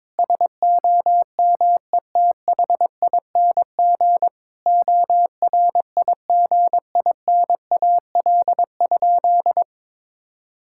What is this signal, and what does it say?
Morse code, heard as a keyed tone, about 22 wpm, 700 Hz. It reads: SOMETHING ORIGINAL?
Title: Sentences from Top 1000 Words - Review Part 6 - 22wpm